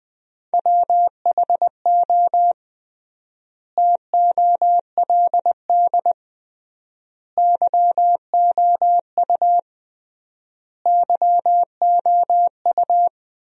Morse code, heard as a keyed tone, about 20 wpm, 700 Hz. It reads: WHO TOLD YOU YOU
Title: 3x Word-Spacing - Sentences from Top 400 Words - 20wpm